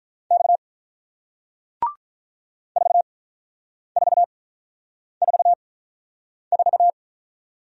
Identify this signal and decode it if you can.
Morse code, heard as a keyed tone, about 47 wpm, spaced as for 9 wpm, 700 Hz. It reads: X 4444